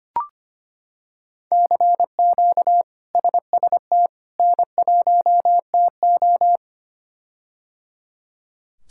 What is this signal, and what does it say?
Morse code, heard as a keyed tone, about 25 wpm, 700 Hz. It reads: CQ SST N1TO
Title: K1USN SST QSO Practice - 25wpm